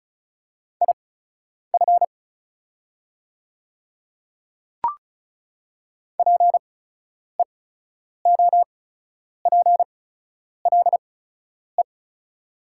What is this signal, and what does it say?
Morse code, heard as a keyed tone, about 36 wpm, spaced as for 10 wpm, 700 Hz. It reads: IF PEOPLE
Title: 8x Character Spacing - Top 100 Words - 35wpm